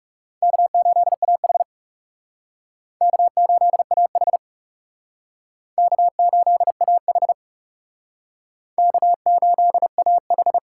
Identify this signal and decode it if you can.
Morse code, heard as a keyed tone, about 40 wpm, 700 Hz. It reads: K8AH K8AH K8AH K8AH